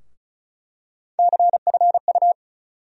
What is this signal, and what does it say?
Morse code, heard as a keyed tone, about 35 wpm, 700 Hz. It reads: CFU
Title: USA Call Sign Postfixes - 3 Letters 35wpm